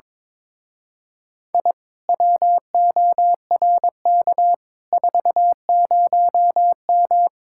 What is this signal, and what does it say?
Morse code, heard as a keyed tone, about 23 wpm, 700 Hz. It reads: I WORK 40M
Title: Speed-Racing Common QSO Phrases Encore - 22wpm